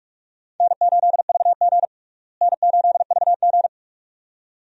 Morse code, heard as a keyed tone, about 41 wpm, 700 Hz. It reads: N8VG N8VG
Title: USA Call Signs - 4 Characters - Repeated 2x - 45wpm